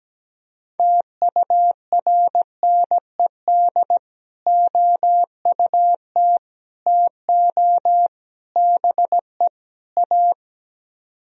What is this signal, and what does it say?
Morse code, heard as a keyed tone, about 17 wpm, 700 Hz. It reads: TURNED OUT TO BE A
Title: Speed-Racing - Sets of 5 Words 17wpm